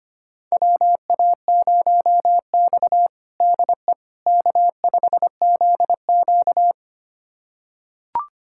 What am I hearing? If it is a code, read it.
Morse code, heard as a keyed tone, about 25 wpm, 700 Hz. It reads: WA0X DE K5ZQ
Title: USA Callsign DE Callsign - 4 Characters - 25wpm